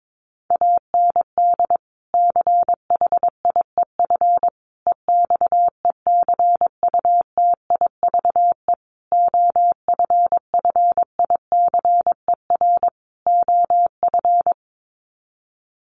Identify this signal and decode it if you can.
Morse code, heard as a keyed tone, about 22 wpm, 700 Hz. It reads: AND CHIEF EXECUTIVE OFFICER OF
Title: Sets of 5 Words Encore2 22wpm